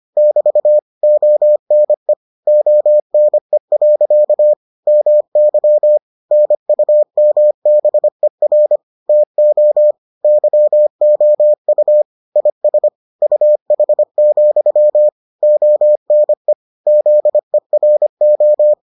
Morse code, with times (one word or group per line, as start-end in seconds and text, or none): 0.2-0.8 <BT>
1.0-2.1 ONE
2.5-4.5 ONE.
4.9-6.0 MY
6.3-8.8 NUMBER
9.1-9.9 TO
10.2-12.0 YOU
12.4-12.9 IS
13.2-15.1 UH,
15.4-16.5 ONE
16.9-18.7 ZERO